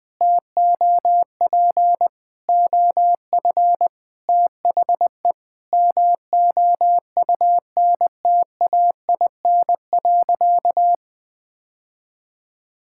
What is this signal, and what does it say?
Morse code, heard as a keyed tone, about 20 wpm, 700 Hz. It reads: TOP OF THE MOUNTAIN.
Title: Sentences with 3-Word Phrases - 20wpm